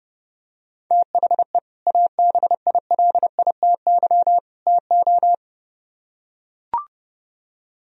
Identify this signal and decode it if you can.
Morse code, heard as a keyed tone, about 30 wpm, 700 Hz. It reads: THE ABILITY TO